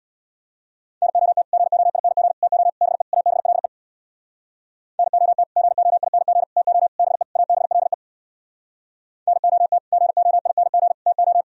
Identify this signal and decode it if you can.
Morse code, heard as a keyed tone, about 64 wpm, 700 Hz. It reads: NOT GOING TO BE ABLE NOT GOING TO BE ABLE NOT GOING TO